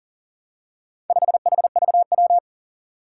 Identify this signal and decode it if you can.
Morse code, heard as a keyed tone, about 40 wpm, 700 Hz. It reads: 5HVW
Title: Four Challenging Letter-Number Combinations 40wpm